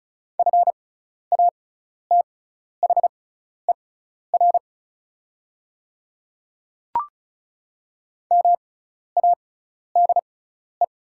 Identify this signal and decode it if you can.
Morse code, heard as a keyed tone, about 36 wpm, spaced as for 12 wpm, 690 Hz. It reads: FATHER MADE
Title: 6x Character Spacing - Top 200 Words - 35wpm